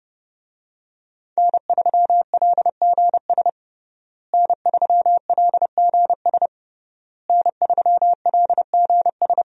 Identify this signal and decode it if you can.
Morse code, heard as a keyed tone, about 30 wpm, 700 Hz. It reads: N3LGS N3LGS N3LGS
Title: USA Call Signs - 5 Characters - Repeated 3x - 30wpm